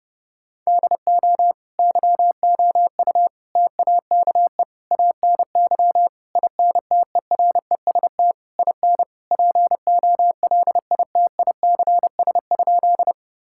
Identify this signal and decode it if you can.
Morse code, heard as a keyed tone, about 30 wpm, 700 Hz. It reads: DO YOU TAKE ANY INTEREST IN POLITICS?
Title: Sentences from Top 1000 Words - Review Part 6 - 30wpm